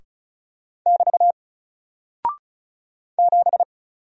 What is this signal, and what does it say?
Morse code, heard as a keyed tone, about 36 wpm, 700 Hz. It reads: <BT> 7
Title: Single Character (Letter:Number:Punctuation:Prosign) Alt 35wpm